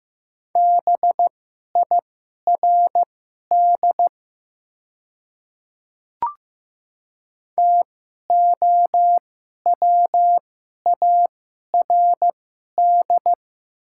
Morse code, heard as a keyed tone, about 15 wpm, spaced as for 11 wpm, 700 Hz. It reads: BIRD TOWARD
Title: 2x Character Spacing - Top 500 Words - 15wpm